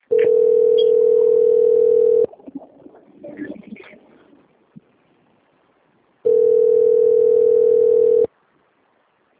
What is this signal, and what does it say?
Ringback tone (North America), 440 + 480 Hz.